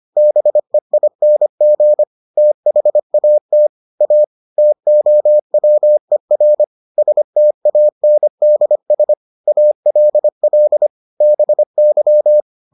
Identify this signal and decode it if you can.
Morse code, heard as a keyed tone, about 25 wpm, 600 Hz. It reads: BEING THAT A TOWER STANDS ALL BY